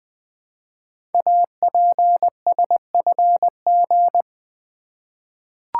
Morse code, heard as a keyed tone, about 21 wpm, 700 Hz. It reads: APSFG
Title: Code Groups - 5 Letters - 20wpm